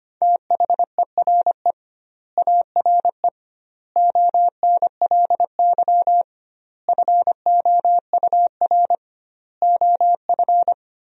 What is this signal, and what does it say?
Morse code, heard as a keyed tone, about 25 wpm, 700 Hz. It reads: THERE ARE ONLY FOUR OF
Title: 2x Word-Spacing - Sentences from Top 300 Words - 25wpm